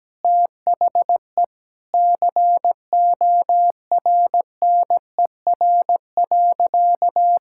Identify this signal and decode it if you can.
Morse code, heard as a keyed tone, about 17 wpm, 700 Hz. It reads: THE CORNER.
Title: Sentences from Top 1600 Words 17wpm